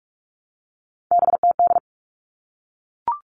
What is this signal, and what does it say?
Morse code, heard as a keyed tone, about 44 wpm, 700 Hz. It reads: BTD
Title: All Cut Numbers - 3-Digit 45wpm